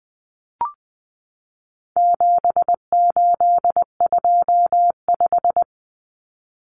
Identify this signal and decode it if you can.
Morse code, heard as a keyed tone, about 20 wpm, 700 Hz. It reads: 7825
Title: Numbers - 4-Digit 20wpm